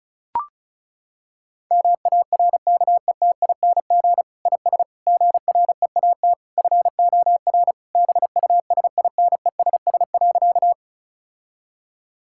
Morse code, heard as a keyed tone, about 36 wpm, 700 Hz. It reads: MARKETING IS GREAT FOR BUSINESS.
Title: Sentences from Top 1000 Words - Review Part 5 - 35wpm